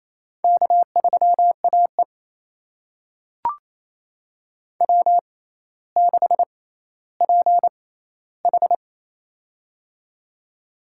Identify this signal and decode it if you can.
Morse code, heard as a keyed tone, about 27 wpm, spaced as for 14 wpm, 700 Hz. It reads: K3AE W6PH
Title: USA Call Signs - 4 Characters - 6x Character Spacing - 28wpm